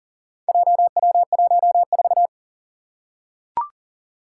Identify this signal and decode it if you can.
Morse code, heard as a keyed tone, about 40 wpm, 700 Hz. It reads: JW14
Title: Four Challenging Letter-Number Combinations 40wpm